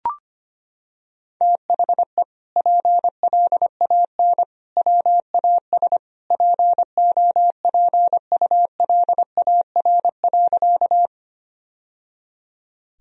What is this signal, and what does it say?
Morse code, heard as a keyed tone, about 25 wpm, 700 Hz. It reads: THE PLAN WAS POPULAR.
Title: Sentences from Top 1000 Words - Review Part 3 - 25wpm